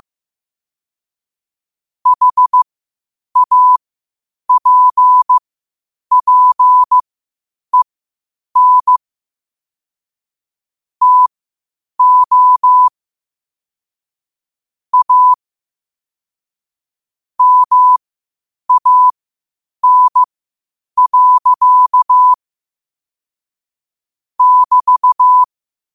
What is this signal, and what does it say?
Morse code, heard as a keyed tone, about 15 wpm, spaced as for 8 wpm, 1000 Hz. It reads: HAPPEN TO A MAN. <BT>
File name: qotdfiles / 8wpm